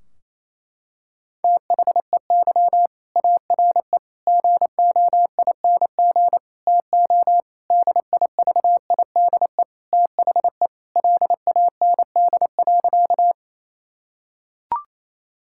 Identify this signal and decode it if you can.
Morse code, heard as a keyed tone, about 27 wpm, 700 Hz. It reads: THEY ARE GOING TO DIVIDE THE LAND.
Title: Sentences from Top 1600 Words 28wpm